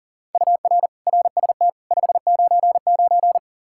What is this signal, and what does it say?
Morse code, heard as a keyed tone, about 40 wpm, 700 Hz. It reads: UR RST 599
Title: Practice QSO - Exchange 1 - Beginning - 40wpm